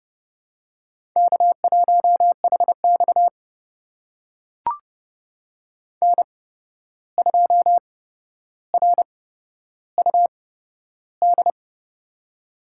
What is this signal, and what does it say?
Morse code, heard as a keyed tone, about 30 wpm, spaced as for 12 wpm, 700 Hz. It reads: K1HX N2RUD
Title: USA Call Signs - 8x Character Spacing - 30wpm